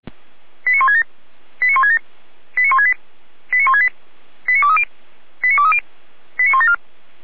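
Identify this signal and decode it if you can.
Five-tone selective call (ZVEI), heard as quick runs of five tones, tones of about 70 ms. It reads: 89167 89167 89168 89168 89230 89230 89174
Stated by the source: ZVEI five-tone selective call 89158 received with an SDR.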